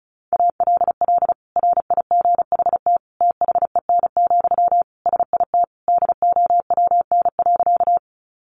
Morse code, heard as a keyed tone, about 34 wpm, 700 Hz. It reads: ALL RIGHT THEN, SIT DOWN.
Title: Sentences from Top 500 Words 35wpm